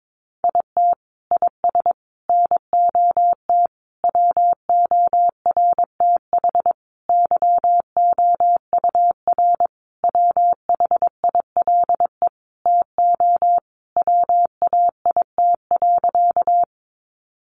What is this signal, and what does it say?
Morse code, heard as a keyed tone, about 22 wpm, 700 Hz. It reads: IT IS NOT WORTH YOUR WHILE TO WAIT.
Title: Sentences from Top 800 Words 22wpm